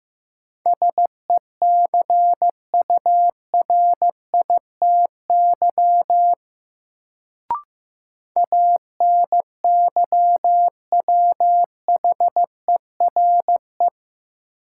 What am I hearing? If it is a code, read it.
Morse code, heard as a keyed tone, about 15 wpm, 700 Hz. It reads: SECURITY ANYWHERE
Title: Sentences from Top 1400 Words - New Word Drill Alt - 15wpm